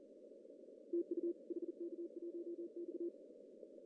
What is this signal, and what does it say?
Morse code, heard as a keyed tone, about 42 wpm, 340 Hz. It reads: TU HK1X
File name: CW / cqww-cw-k5tr